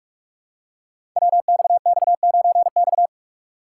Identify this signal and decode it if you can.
Morse code, heard as a keyed tone, about 46 wpm, 700 Hz. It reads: WXX9X
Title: Code Groups - 5 Letter:Numbers - 45wpm